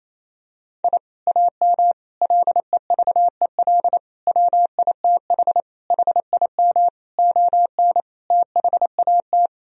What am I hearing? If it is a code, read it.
Morse code, heard as a keyed tone, about 28 wpm, 700 Hz. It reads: I AM LEVEL WITH HIM ON THAT
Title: Sentences from Top 1000 Words - Review Part 4 - 28wpm